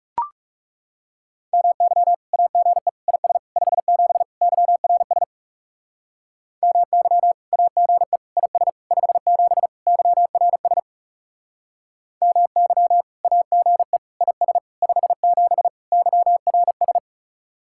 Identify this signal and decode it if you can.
Morse code, heard as a keyed tone, about 39 wpm, 700 Hz. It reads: MY AGE IS 57 YRS MY AGE IS 57 YRS MY AGE IS 57 YRS